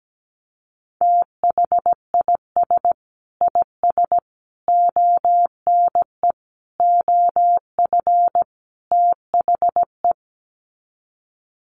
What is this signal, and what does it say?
Morse code, heard as a keyed tone, about 17 wpm, 700 Hz. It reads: THIS IS ONE OF THE
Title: Sets of 5 Words 17wpm